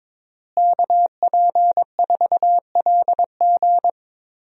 Morse code, heard as a keyed tone, about 21 wpm, 700 Hz. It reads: KP4LG
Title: DXCC-100 Call Signs - 5 Characters - Repeated 2x - 22wpm